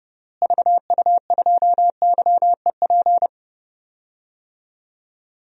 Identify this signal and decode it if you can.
Morse code, heard as a keyed tone, about 30 wpm, 700 Hz. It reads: VU2YEP
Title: DXCC-100 Call Signs - 6 Characters - Repeated 2x - 30wpm